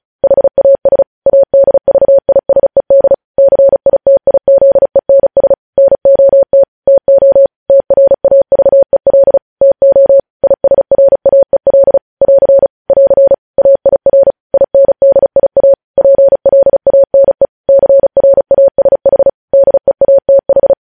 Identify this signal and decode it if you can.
Morse code, heard as a keyed tone, about 36 wpm, 550 Hz. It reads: HAS ADVISED CITIZENS NOT TO TRAVEL TO ISRAEL <AR> <AR> AIR INDIA PLANE CRASH DEATH
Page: news headlines in morse code at 35 wpm